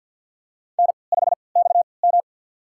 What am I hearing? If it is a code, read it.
Morse code, heard as a keyed tone, about 53 wpm, spaced as for 29 wpm, 700 Hz. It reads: N5XM
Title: USA Call Signs - 4 Characters - 3x Character Spacing - 50wpm